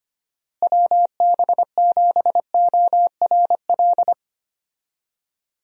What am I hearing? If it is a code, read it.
Morse code, heard as a keyed tone, about 25 wpm, 700 Hz. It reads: WB7ORL
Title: USA Call Signs - 6 Characters - Repeated 3x - 25wpm